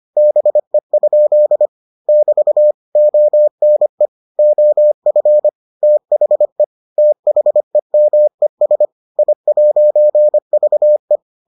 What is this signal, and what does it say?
Morse code, heard as a keyed tone, about 25 wpm, 600 Hz. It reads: BE? <BT> ONE OF THE THEMES I'VE